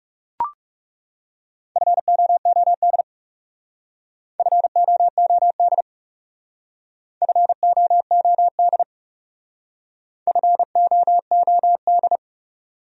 Morse code, heard as a keyed tone, about 36 wpm, 700 Hz. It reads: FOOD FOOD FOOD FOOD
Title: Speed-Racing Top 300 Words 30wpm